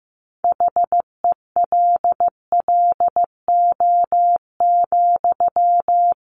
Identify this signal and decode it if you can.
Morse code, heard as a keyed tone, about 15 wpm, 700 Hz. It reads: HELLO,
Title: Sentences from Top 1100 Words 15wpm